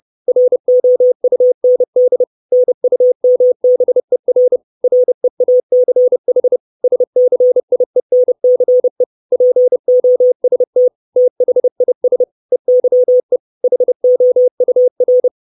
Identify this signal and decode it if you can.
Morse code, heard as a keyed tone, about 30 wpm, 500 Hz. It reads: ROUND NUMBER REACH SCIENCE POST THIS EYE HOUR
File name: 632 Words 30WPM